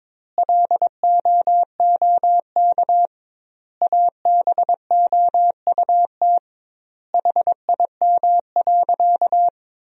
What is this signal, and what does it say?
Morse code, heard as a keyed tone, about 22 wpm, 700 Hz. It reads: LOOK ABOUT HIM.
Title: 2x Word-Spacing - Sentences from Top 200 Words - 22wpm